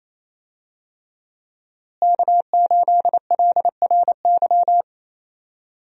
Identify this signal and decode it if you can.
Morse code, heard as a keyed tone, about 27 wpm, 700 Hz. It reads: K8LRY
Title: USA Call Signs - 5 Characters - Repeated 3x - 28wpm